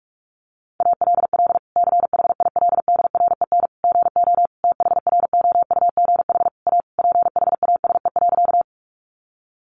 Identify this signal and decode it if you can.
Morse code, heard as a keyed tone, about 44 wpm, 700 Hz. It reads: ALL CHILDREN GO THROUGH A PHASE.
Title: Sentences from Top 1200 Words 45wpm